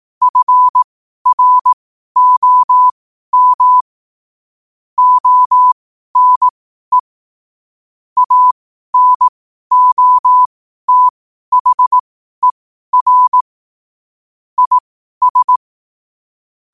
Morse code, heard as a keyed tone, about 18 wpm, spaced as for 12 wpm, 1000 Hz. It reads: FROM ONE ANOTHER IS
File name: qotdfiles / 12wpm